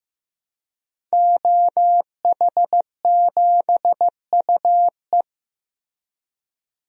Morse code, heard as a keyed tone, about 15 wpm, 700 Hz. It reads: OH7UE
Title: DXCC-100 Call Signs - 5 Characters - Repeated 3x - 15wpm